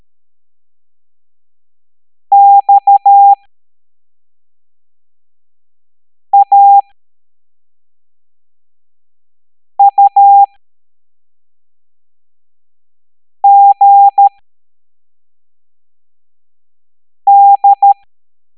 Morse code, heard as a keyed tone, about 13 wpm, spaced as for 3 wpm, 800 Hz. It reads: XAUGD